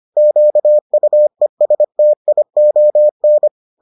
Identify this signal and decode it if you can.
Morse code, heard as a keyed tone, about 25 wpm, 600 Hz. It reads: QUESTION